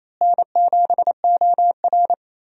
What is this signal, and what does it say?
Morse code, heard as a keyed tone, about 27 wpm, 700 Hz. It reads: N7OR
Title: USA Call Signs - 4 Characters - Repeated 3x - 28wpm